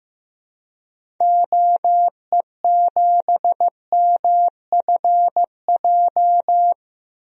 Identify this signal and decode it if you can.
Morse code, heard as a keyed tone, about 16 wpm, 700 Hz. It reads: OE7MFJ
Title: DXCC-100 Call Signs - 6 Characters - 15wpm